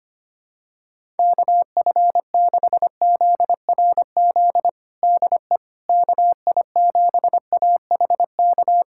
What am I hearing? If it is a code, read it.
Morse code, heard as a keyed tone, about 25 wpm, 700 Hz. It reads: KF6ZRZ DE KI7AHK
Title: USA Callsign DE Callsign 25wpm